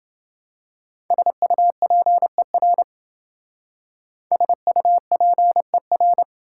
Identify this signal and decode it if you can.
Morse code, heard as a keyed tone, about 27 wpm, 700 Hz. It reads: SUPER SUPER